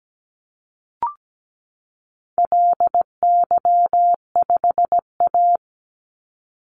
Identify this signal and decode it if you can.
Morse code, heard as a keyed tone, about 17 wpm, 700 Hz. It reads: LY5A
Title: Call Signs Repeated 2x - 17wpm